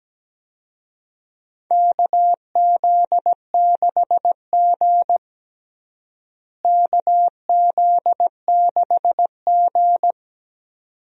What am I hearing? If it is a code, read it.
Morse code, heard as a keyed tone, about 17 wpm, 700 Hz. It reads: KZ6G KZ6G
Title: USA Call Signs - 4 Characters - Repeated 2x - 17wpm